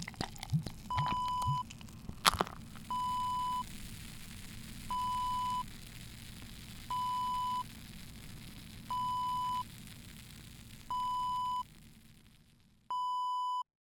There is no code.